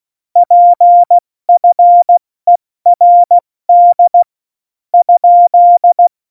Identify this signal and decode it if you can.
Morse code, heard as a keyed tone, about 16 wpm, 700 Hz. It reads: PFERD ?
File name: Morsekurs / Witze / Witz 03.Dezember 2019, 70 F80